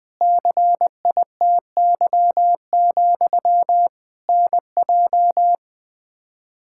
Morse code, heard as a keyed tone, about 21 wpm, 700 Hz. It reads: CITY, NJ
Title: K1USN SST QSO Practice 20wpm